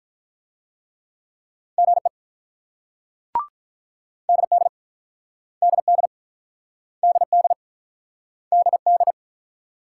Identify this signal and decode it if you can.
Morse code, heard as a keyed tone, about 50 wpm, 700 Hz. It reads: GE DD DD DD DD